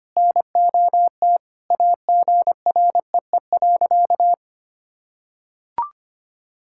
Morse code, heard as a keyed tone, about 25 wpm, 700 Hz. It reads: NOT AGREE.
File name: practice / 25wpm / Sentences with 4-Word Phrases - 25wpm